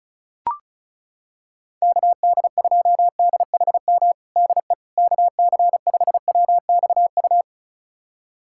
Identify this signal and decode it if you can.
Morse code, heard as a keyed tone, about 34 wpm, 700 Hz. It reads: KD2DHM DE KC5WXU
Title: USA Callsign DE Callsign - 6 Characters - 35wpm